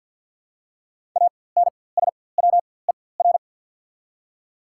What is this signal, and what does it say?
Morse code, heard as a keyed tone, about 53 wpm, spaced as for 24 wpm, 690 Hz. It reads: ANSWER
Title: 4x Character Spacing - Top 300 Words - 50wpm